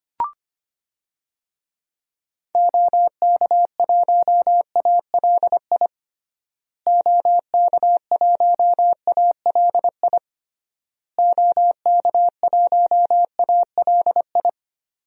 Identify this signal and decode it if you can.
Morse code, heard as a keyed tone, about 25 wpm, 700 Hz. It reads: OK1ALI OK1ALI OK1ALI
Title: DXCC-100 Call Signs - 6 Characters - Repeated 3x - 25wpm